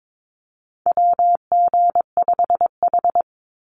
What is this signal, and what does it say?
Morse code, heard as a keyed tone, about 22 wpm, 700 Hz. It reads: WG5H
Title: Call Signs Repeated 3x - 22wpm